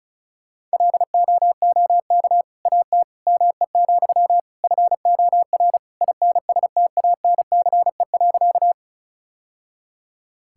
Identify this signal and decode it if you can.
Morse code, heard as a keyed tone, about 36 wpm, 700 Hz. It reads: LOOK AT ME, FOR INSTANCE.
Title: Sentences from Top 1000 Words - Review Part 5 - 35wpm